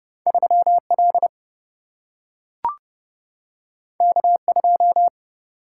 Morse code, heard as a keyed tone, about 31 wpm, 700 Hz. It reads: 3 L K 2
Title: Single-Letter Single-Number 30wpm